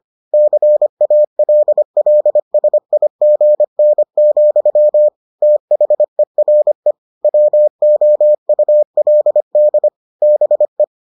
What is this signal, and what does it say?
Morse code, heard as a keyed tone, about 25 wpm, 600 Hz. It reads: CALLSIGN, THERE WOULD BE